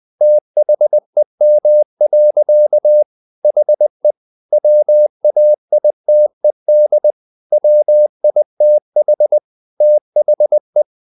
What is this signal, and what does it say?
Morse code, heard as a keyed tone, about 20 wpm, 600 Hz. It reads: THEM. HE WAITED WITH THE
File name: the old man and the sea / the old man and the sea part 1 20wpm